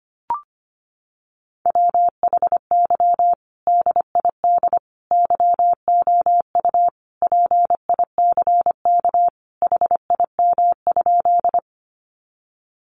Morse code, heard as a keyed tone, about 25 wpm, 700 Hz. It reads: WHY DID YOU PICK HIM?
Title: Sentences from Top 1000 Words 25wpm